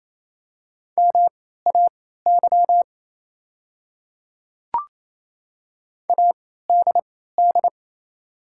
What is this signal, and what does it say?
Morse code, heard as a keyed tone, about 28 wpm, spaced as for 16 wpm, 700 Hz. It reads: MAY ADD
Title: 3x Character Spacing - Top 300 Words - 28wpm